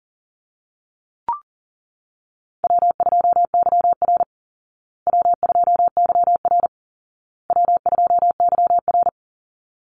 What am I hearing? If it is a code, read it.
Morse code, heard as a keyed tone, about 38 wpm, 700 Hz. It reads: W2YR W2YR W2YR
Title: Call Signs Repeated 2x - 40wpm